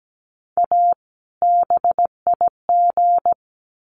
Morse code, heard as a keyed tone, about 17 wpm, 700 Hz. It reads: A BIG
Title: Sets of 2 Words 17wpm